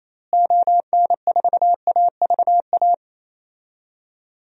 Morse code, heard as a keyed tone, about 28 wpm, 700 Hz. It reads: ON4AVA
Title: DXCC-100 Call Signs - 6 Characters - Repeated 3x - 28wpm